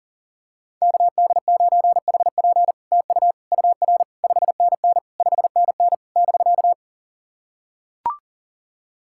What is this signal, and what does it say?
Morse code, heard as a keyed tone, about 40 wpm, 700 Hz. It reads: KD9HP TU UR 5NN 5NN <BK>
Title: SOTA QSO Practice 40wpm